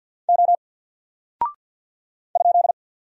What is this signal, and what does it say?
Morse code, heard as a keyed tone, about 48 wpm, 700 Hz. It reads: O ?
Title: Single Character (Letter:Number:Punctuation:Prosign) 50wpm